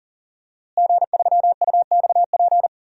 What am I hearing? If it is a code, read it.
Morse code, heard as a keyed tone, about 37 wpm, 700 Hz. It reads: G3UXP
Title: DXCC-100 Call Signs - 5 Characters - Repeated 2x - 40wpm